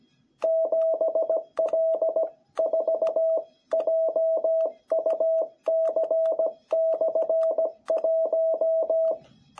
Morse code, heard as a keyed tone, about 17 wpm, 660 Hz. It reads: 7L4JUC/1
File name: jh8cbh / 2021cw